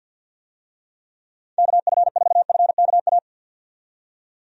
Morse code, heard as a keyed tone, about 50 wpm, 700 Hz. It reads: KF4FKA
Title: USA Call Signs - 6 Characters - 50wpm